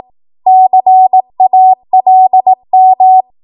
Morse code, heard as a keyed tone, about 18 wpm, 750 Hz. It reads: CALM
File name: Archive / 18 WPM / 18WPM